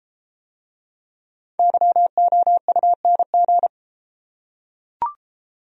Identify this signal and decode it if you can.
Morse code, heard as a keyed tone, about 33 wpm, 700 Hz. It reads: YOUNG